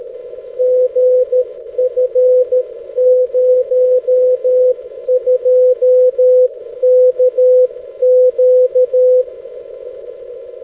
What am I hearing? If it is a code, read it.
Morse code, heard as a keyed tone, about 13 wpm, 500 Hz. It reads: GF02KQ
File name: 6m-lu6ee-b